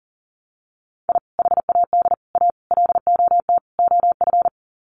Morse code, heard as a keyed tone, about 37 wpm, 700 Hz. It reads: I HAD A LOT OF
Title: Sets of 5 Words Encore 40wpm